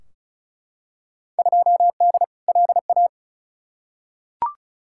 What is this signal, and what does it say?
Morse code, heard as a keyed tone, about 34 wpm, 700 Hz. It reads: 2D LA
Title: Field Day Exchanges - 35wpm